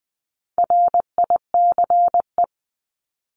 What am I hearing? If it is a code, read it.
Morse code, heard as a keyed tone, about 20 wpm, 700 Hz. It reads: RICE
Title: Top 1501-1750 Words 20wpm